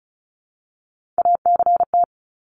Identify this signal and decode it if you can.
Morse code, heard as a keyed tone, about 34 wpm, 700 Hz. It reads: ACT
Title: Top 300 Words 35wpm